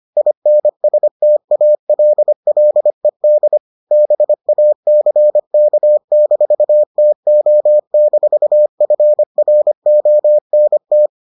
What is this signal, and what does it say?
Morse code, heard as a keyed tone, about 25 wpm, 600 Hz. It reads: INSTALLED BACK-TO-FRONT